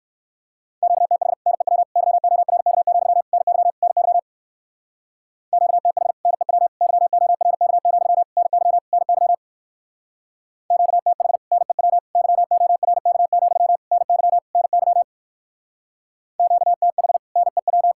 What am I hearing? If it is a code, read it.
Morse code, heard as a keyed tone, about 58 wpm, 700 Hz. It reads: QTH NEW YORK, NY NY QTH NEW YORK, NY NY QTH NEW YORK, NY NY QTH NEW